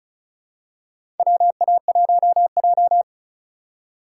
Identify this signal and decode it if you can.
Morse code, heard as a keyed tone, about 36 wpm, 700 Hz. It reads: WA1J
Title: USA Call Signs - 4 Characters - 3x Character Spacing - 35wpm